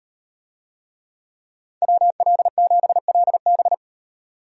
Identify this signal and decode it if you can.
Morse code, heard as a keyed tone, about 38 wpm, 700 Hz. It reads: WL7LB